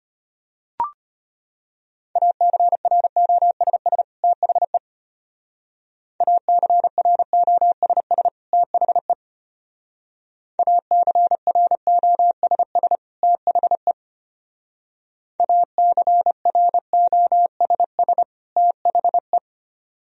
Morse code, heard as a keyed tone, about 32 wpm, 700 Hz. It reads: ACROSS THE ACROSS THE ACROSS THE ACROSS THE